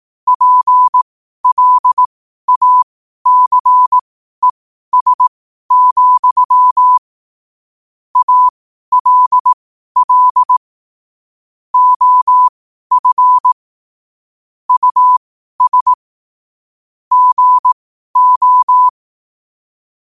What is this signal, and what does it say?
Morse code, heard as a keyed tone, about 18 wpm, spaced as for 12 wpm, 1000 Hz. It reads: PLACES, ALL OF US GO
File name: qotdfiles / 12wpm